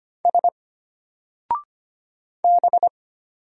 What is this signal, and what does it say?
Morse code, heard as a keyed tone, about 25 wpm, 700 Hz. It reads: S B
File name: Single-Letter-Number-25wpm